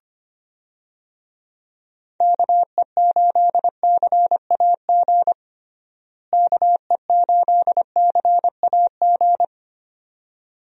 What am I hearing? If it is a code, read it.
Morse code, heard as a keyed tone, about 25 wpm, 700 Hz. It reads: KE8CAG KE8CAG
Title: USA Call Signs - 6 Characters - Repeated 2x - 25wpm